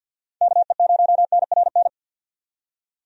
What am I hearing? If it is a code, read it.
Morse code, heard as a keyed tone, about 48 wpm, 700 Hz. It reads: KE0NRN
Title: USA Call Signs - 6 Characters - 50wpm